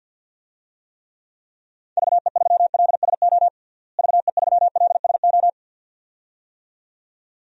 Morse code, heard as a keyed tone, about 53 wpm, 700 Hz. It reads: VE3LSO VE3LSO
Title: DXCC-100 Call Signs - 6 Characters - Repeated 2x - 50wpm